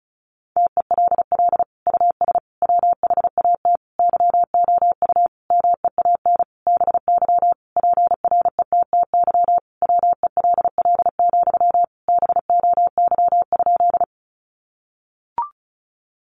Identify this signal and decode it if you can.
Morse code, heard as a keyed tone, about 34 wpm, 700 Hz. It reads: TELL US WHAT YOU MEAN BY PRETTY WELL, BOY?
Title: Sentences from Top 1000 Words 35wpm